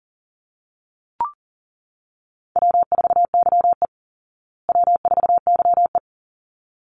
Morse code, heard as a keyed tone, about 40 wpm, 700 Hz. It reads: W4YE W4YE
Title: Call Signs Repeated 3x - 40wpm